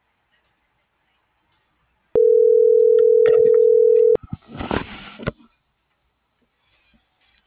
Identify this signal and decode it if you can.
Ringback tone (North America), 440 + 480 Hz.